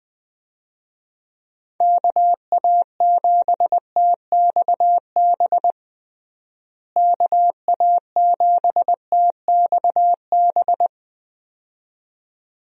Morse code, heard as a keyed tone, about 20 wpm, 700 Hz. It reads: KA7TXB KA7TXB
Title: USA Call Signs - 6 Characters - Repeated 2x - 20wpm